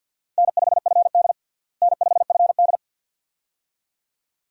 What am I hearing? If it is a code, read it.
Morse code, heard as a keyed tone, about 51 wpm, 700 Hz. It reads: N5FD N5FD
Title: USA Call Signs - 4 Characters - Repeated 2x - 50wpm